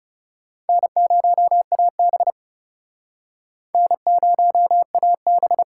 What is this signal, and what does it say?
Morse code, heard as a keyed tone, about 35 wpm, 700 Hz. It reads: N0AB N0AB
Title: Speed Racing - USA 1x2 Call Signs 30wpm